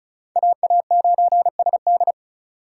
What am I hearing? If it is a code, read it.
Morse code, heard as a keyed tone, about 44 wpm, 700 Hz. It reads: AA9SD